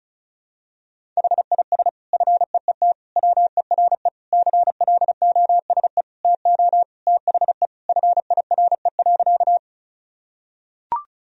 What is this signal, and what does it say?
Morse code, heard as a keyed tone, about 34 wpm, 700 Hz. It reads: HIS FEET WERE CLOSE TO THE FIRE.
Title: Sentences from Top 1000 Words - Review Part 1 - 35wpm